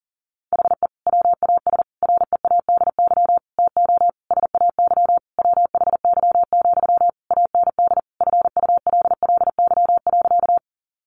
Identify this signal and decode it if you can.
Morse code, heard as a keyed tone, about 40 wpm, 700 Hz. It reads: HE WAS READY TO SAY WHY, AND FULLY.
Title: Sentences from Top 1300 Words 40wpm